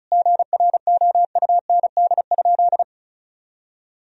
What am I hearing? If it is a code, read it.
Morse code, heard as a keyed tone, about 36 wpm, 700 Hz. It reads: GROUND?
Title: Sentences from Top 1000 Words - Review Part 2 - 35wpm